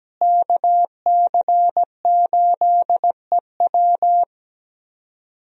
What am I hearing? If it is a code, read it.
Morse code, heard as a keyed tone, about 17 wpm, 700 Hz. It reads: KC8EW
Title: USA Call Signs - 5 Characters - Repeated 3x - 17wpm